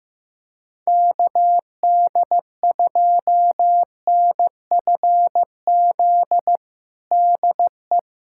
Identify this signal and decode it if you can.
Morse code, heard as a keyed tone, about 15 wpm, 700 Hz. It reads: KD2NFZ DE
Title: USA Callsign DE Callsign - 6 Characters - 15wpm